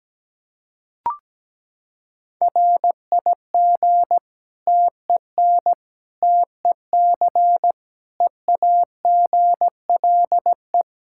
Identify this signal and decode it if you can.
Morse code, heard as a keyed tone, about 17 wpm, 700 Hz. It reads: RIG TEN TEC EAGLE
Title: Practice QSO - Exchange 2 - Info - 17wpm